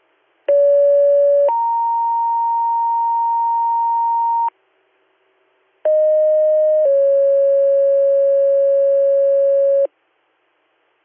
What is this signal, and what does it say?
Two-tone page, heard as a one-second tone, then a three-second tone, about 570 then 930 Hz, about 620 then 550 Hz. It is heard from 0.5 to 9.9 s.